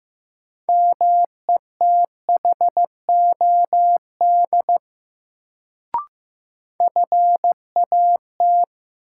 Morse code, heard as a keyed tone, about 15 wpm, 700 Hz. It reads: METHOD FAT
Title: Sentences from Top 600 Words - New Word Drill - 15wpm